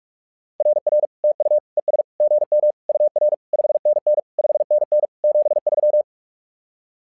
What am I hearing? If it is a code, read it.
Morse code, heard as a keyed tone, about 46 wpm, 600 Hz. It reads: RR TU ES GM UR 5NN 5NN 73
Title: SOTA QSO Practice 45wpm